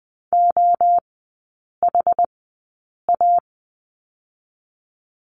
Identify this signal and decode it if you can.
Morse code, heard as a keyed tone, about 20 wpm, spaced as for 8 wpm, 700 Hz. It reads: OHA
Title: 172 Lesson 28e - Number 9 - Three Characters 20wpm